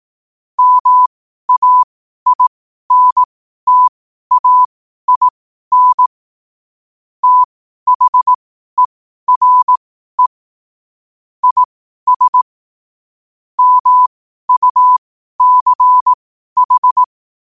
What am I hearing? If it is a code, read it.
Morse code, heard as a keyed tone, about 18 wpm, spaced as for 12 wpm, 1000 Hz. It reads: MAINTAIN THERE IS MUCH
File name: qotdfiles / 12wpm